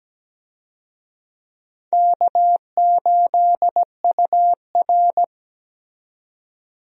Morse code, heard as a keyed tone, about 17 wpm, 700 Hz. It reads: K8UR
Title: USA Call Signs - 4 Characters - 17wpm